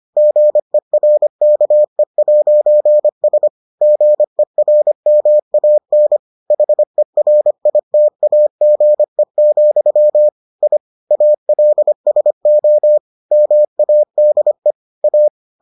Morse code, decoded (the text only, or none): GERKE'S GERMAN HERITAGE, I ALSO MADE A